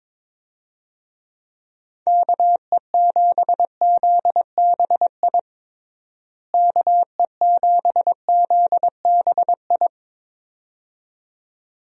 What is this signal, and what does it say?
Morse code, heard as a keyed tone, about 22 wpm, 700 Hz. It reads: KE7ZBI KE7ZBI
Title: USA Call Signs - 6 Characters - Repeated 2x - 22wpm